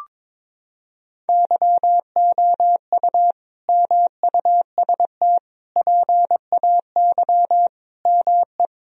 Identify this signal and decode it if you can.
Morse code, heard as a keyed tone, about 22 wpm, 700 Hz. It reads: YOU MUST PAY ME
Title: Sentences from Top 1000 Words - Review Part 2 - 22wpm